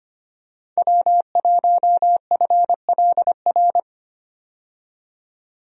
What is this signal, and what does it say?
Morse code, heard as a keyed tone, about 25 wpm, 700 Hz. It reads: W1FLR
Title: USA Call Signs - 5 Characters - 25wpm